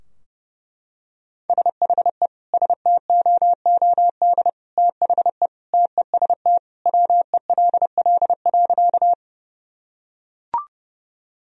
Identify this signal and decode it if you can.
Morse code, heard as a keyed tone, about 32 wpm, 700 Hz. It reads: SHE STOOD THE TEST WELL.
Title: Sentences from Top 1000 Words - Review Part 3 - 30wpm